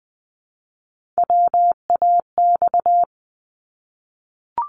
Morse code, heard as a keyed tone, about 19 wpm, 700 Hz. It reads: WAX
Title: Multi-letter Phonemes Repeated 3x 20wpm